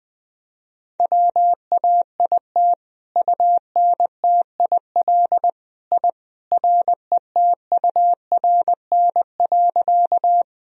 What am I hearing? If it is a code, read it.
Morse code, heard as a keyed tone, about 20 wpm, 700 Hz. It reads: WAIT UNTIL I RETURN.
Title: Sentences from Top 1000 Words - Review Part 1 - 20wpm